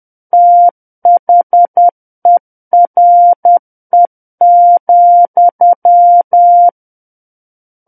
Morse code, heard as a keyed tone, about 10 wpm, 700 Hz. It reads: THERE,
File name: Book One / 01-Chapter 1 - 10 WPM